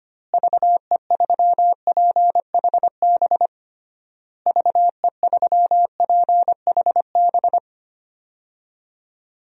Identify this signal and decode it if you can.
Morse code, heard as a keyed tone, about 25 wpm, 700 Hz. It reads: VE3PHB VE3PHB
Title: DXCC-100 Call Signs - 6 Characters - Repeated 3x - 25wpm